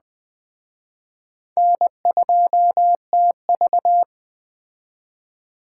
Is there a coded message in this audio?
Yes: Morse code from 1.6 to 4.0 s, heard as a keyed tone, about 19 wpm, 700 Hz. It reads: N2TV